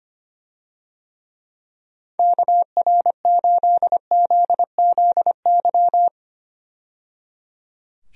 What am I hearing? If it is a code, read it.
Morse code, heard as a keyed tone, about 25 wpm, 700 Hz. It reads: KR8ZZY
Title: USA Call Signs - 6 Characters - 25wpm